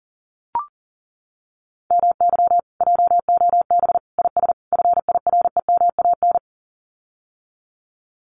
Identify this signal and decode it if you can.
Morse code, heard as a keyed tone, about 40 wpm, 700 Hz. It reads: MY JOB IS FIREMAN
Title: Common QSO Phrases Encore 40wpm